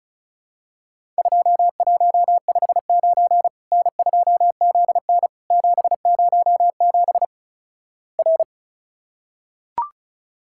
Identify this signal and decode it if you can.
Morse code, heard as a keyed tone, about 36 wpm, 700 Hz. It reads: 2159 N2ZN 707 R
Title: Worked All Europe DX Contest - QTC Exchange - 35wpm